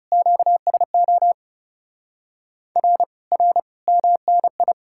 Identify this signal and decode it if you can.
Morse code, heard as a keyed tone, about 37 wpm, 700 Hz. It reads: QSO R R MNI
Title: Speed-Racing Common QSO Phrases - 30wpm